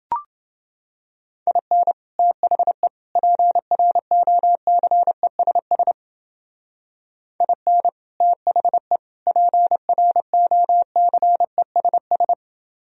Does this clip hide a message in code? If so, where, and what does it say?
Yes: Morse code from 1.5 to 12.3 s, heard as a keyed tone, about 28 wpm, 700 Hz. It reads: IN THE PROCESS IN THE PROCESS